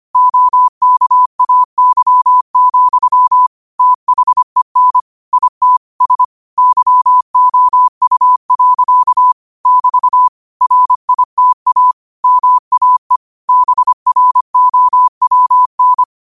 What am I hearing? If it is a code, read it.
Morse code, heard as a keyed tone, about 25 wpm, 1000 Hz. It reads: OKAY, THEN IT S YOU. <BT> RITA MAE BROWN
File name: qotdfiles / 25wpm